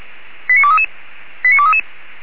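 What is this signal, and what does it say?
Five-tone selective call (ZVEI), heard as quick runs of five tones, tones of about 70 ms. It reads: 89233 89233
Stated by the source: Received on an SDR.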